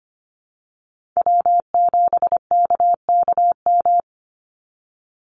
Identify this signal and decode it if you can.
Morse code, heard as a keyed tone, about 25 wpm, 700 Hz. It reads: W7KKM
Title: NA SOTA Chaser Callsigns 25wpm